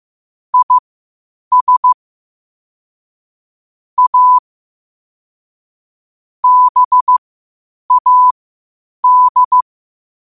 Morse code, heard as a keyed tone, about 15 wpm, spaced as for 8 wpm, 1000 Hz. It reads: IS A BAD
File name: qotdfiles / 8wpm